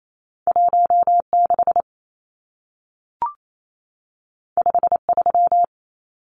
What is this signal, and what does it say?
Morse code, heard as a keyed tone, about 28 wpm, 700 Hz. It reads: 16 53
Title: Numbers - 2-Digit 28wpm